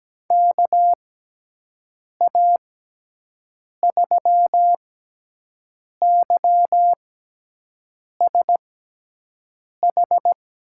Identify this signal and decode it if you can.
Morse code, heard as a keyed tone, about 17 wpm, spaced as for 6 wpm, 700 Hz. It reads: KA3YSH
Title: USA Call Signs - 6x Character Spacing - 17wpm